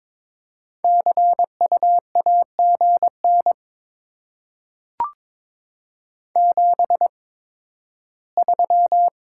Morse code, heard as a keyed tone, about 22 wpm, spaced as for 13 wpm, 700 Hz. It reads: CU AGN 73
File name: Top 20 QSO Elements - 8x Character Spacing - 22wpm